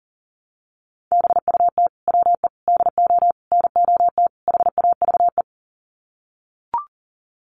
Morse code, heard as a keyed tone, about 40 wpm, 700 Hz. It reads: BUT WE DO NOT HAVE
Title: Sets of 5 Words Encore2 40wpm